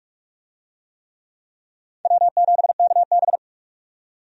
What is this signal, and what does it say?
Morse code, heard as a keyed tone, about 46 wpm, 700 Hz. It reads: W7KB
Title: USA Call Signs - 4 Characters - 45wpm